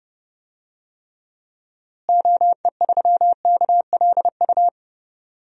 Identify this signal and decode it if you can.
Morse code, heard as a keyed tone, about 29 wpm, 700 Hz. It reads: OE3KLU
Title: DXCC-100 Call Signs - 6 Characters - Repeated 3x - 30wpm